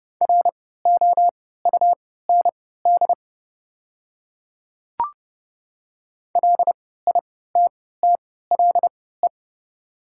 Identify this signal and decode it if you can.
Morse code, heard as a keyed tone, about 30 wpm, spaced as for 17 wpm, 700 Hz. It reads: ROUND LITTLE
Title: 3x Character Spacing - Top 300 Words - 30wpm